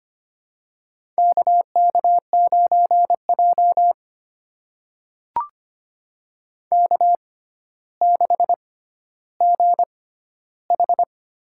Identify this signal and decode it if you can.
Morse code, heard as a keyed tone, about 25 wpm, spaced as for 13 wpm, 700 Hz. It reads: KK9J K6GH
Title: USA Call Signs - 4 Characters - 6x Character Spacing - 25wpm